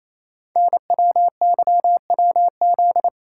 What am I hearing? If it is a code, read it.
Morse code, heard as a keyed tone, about 27 wpm, 700 Hz. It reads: NWYWZ
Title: Code Groups - 5 Letters - 28wpm